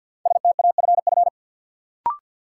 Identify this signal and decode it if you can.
Morse code, heard as a keyed tone, about 53 wpm, 700 Hz. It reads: STAFF
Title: Sentences from Top 1200 Words - New Word Drill Alt - 50wpm